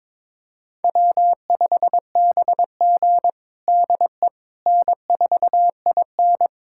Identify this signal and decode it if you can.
Morse code, heard as a keyed tone, about 22 wpm, 700 Hz. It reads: W5BG DE N4IN
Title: USA Callsign DE Callsign - 4 Characters - 22wpm